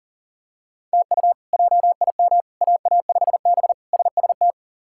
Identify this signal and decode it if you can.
Morse code, heard as a keyed tone, about 37 wpm, 700 Hz. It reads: TU JIM AA5B SST
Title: K1USN SST QSO Practice - 40wpm